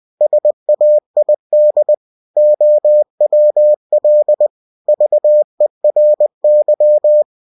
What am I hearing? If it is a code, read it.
Morse code, heard as a keyed tone, about 20 wpm, 600 Hz. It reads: SAID OWL VERY
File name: cw-books / winnie-the-pooh / winnie-20wpm-page-0030